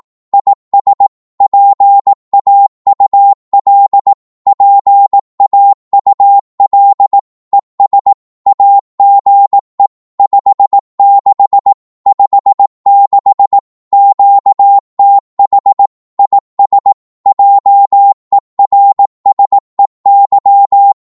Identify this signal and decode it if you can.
Morse code, heard as a keyed tone, about 18 wpm, 800 Hz. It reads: IS PAUL PAUL ES AGE 56 56 QTH IS JERSEY
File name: Morse / qso-18wpm